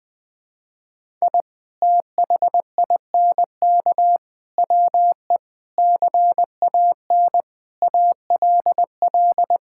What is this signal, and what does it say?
Morse code, heard as a keyed tone, about 20 wpm, 700 Hz. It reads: I THINK WE CAN ALL
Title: Sentences with 7-Word Phrases - Spoken Beginning - 20wpm